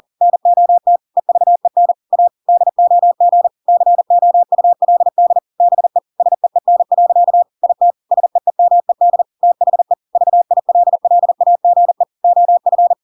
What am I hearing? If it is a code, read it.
Morse code, heard as a keyed tone, about 40 wpm, 700 Hz. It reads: NOT EVEN A DOG COULD BE SEEN. IT SEEMED THE VILLAGE OF